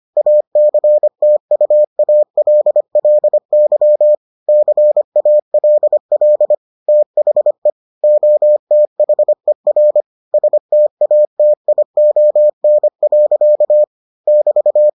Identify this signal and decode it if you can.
Morse code, heard as a keyed tone, about 25 wpm, 600 Hz. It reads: ACTUALLY CALL THE OTHER STATION. <BT>